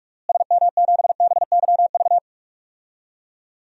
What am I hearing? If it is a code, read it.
Morse code, heard as a keyed tone, about 46 wpm, 700 Hz. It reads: SM7BYV
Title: DXCC-100 Call Signs - 6 Characters - Repeated 3x - 45wpm